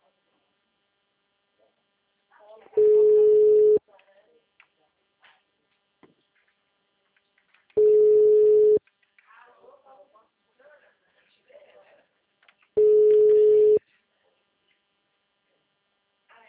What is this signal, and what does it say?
Ringback tone (Europe, 425 Hz), with speech.